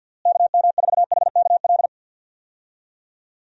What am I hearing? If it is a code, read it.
Morse code, heard as a keyed tone, about 50 wpm, 700 Hz. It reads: KM4HKL
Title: USA Call Signs - 6 Characters - Repeated 2x - 50wpm